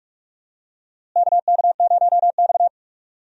Morse code, heard as a keyed tone, about 42 wpm, 700 Hz. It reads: KK0X
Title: USA Call Signs - 4 Characters - Repeated 3x - 45wpm